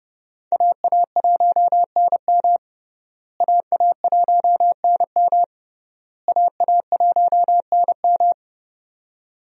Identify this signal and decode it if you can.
Morse code, heard as a keyed tone, about 30 wpm, 700 Hz. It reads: AA1NM AA1NM AA1NM
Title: USA Call Signs - 5 Characters - Repeated 3x - 30wpm